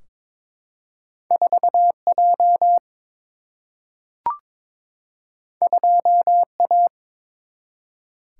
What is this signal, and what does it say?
Morse code, heard as a keyed tone, about 23 wpm, 700 Hz. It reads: 4J 2A
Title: Single-Letter Single-Number 22wpm